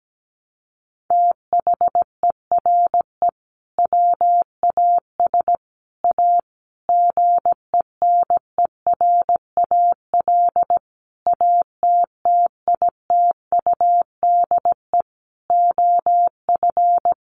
Morse code, heard as a keyed tone, about 17 wpm, 700 Hz. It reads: THERE WAS A GENERAL ATTITUDE OF
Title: Sentences from Top 1300 Words 17wpm